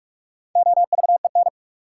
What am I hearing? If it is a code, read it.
Morse code, heard as a keyed tone, about 46 wpm, 700 Hz. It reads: OVEN
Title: Sentences from Top 1000 Words - Review Part 4 - New Word Drill - 45wpm